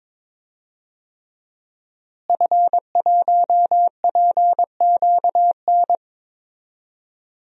Morse code, heard as a keyed tone, about 22 wpm, 700 Hz. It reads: F1PQN
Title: DXCC-100 Call Signs - 5 Characters - Repeated 3x - 22wpm